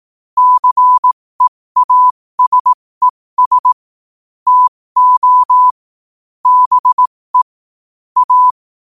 Morse code, heard as a keyed tone, about 18 wpm, spaced as for 15 wpm, 1000 Hz. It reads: CEASES TO BE A
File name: qotdfiles / 15wpm